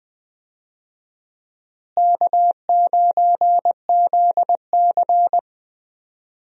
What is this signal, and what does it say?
Morse code, heard as a keyed tone, about 21 wpm, 690 Hz. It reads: K9ZC
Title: USA Call Signs - 4 Characters - Repeated 3x - 20wpm